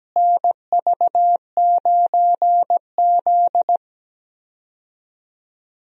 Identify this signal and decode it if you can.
Morse code, heard as a keyed tone, about 17 wpm, 700 Hz. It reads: NV9Z
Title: USA Call Signs - 4 Characters - Repeated 2x - 17wpm